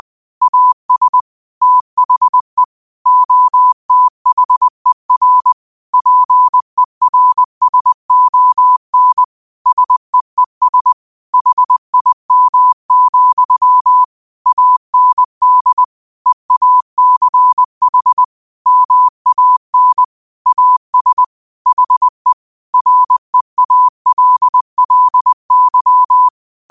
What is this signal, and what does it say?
Morse code, heard as a keyed tone, about 20 wpm, 1000 Hz. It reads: AS THE OTHER PERSON SEES HIM, AND EACH MAN AS HE REALLY
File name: qotdfiles / 20wpm